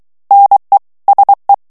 Morse code, heard as a keyed tone, about 23 wpm, 800 Hz. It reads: NE SE
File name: morse mp3 / FRANCAIS TEXTE CLAIR 2 VITESSE 22 MOTS MINUTES